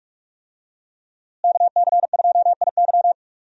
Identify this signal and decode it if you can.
Morse code, heard as a keyed tone, about 46 wpm, 700 Hz. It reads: KC2IY
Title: USA Call Signs - 5 Characters - Repeated 3x - 45wpm